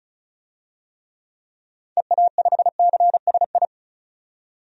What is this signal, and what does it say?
Morse code, heard as a keyed tone, about 36 wpm, 700 Hz. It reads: EA5CSI
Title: DXCC-100 Call Signs - 6 Characters - 35wpm